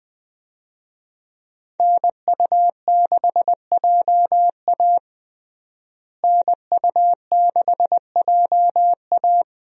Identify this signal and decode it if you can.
Morse code, heard as a keyed tone, about 20 wpm, 700 Hz. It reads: NU6JA NU6JA
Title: USA Call Signs - 5 Characters - Repeated 2x - 20wpm